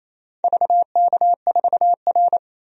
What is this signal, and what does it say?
Morse code, heard as a keyed tone, about 28 wpm, 700 Hz. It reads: VK4R
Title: DXCC-100 Call Signs - 4 Characters - 28wpm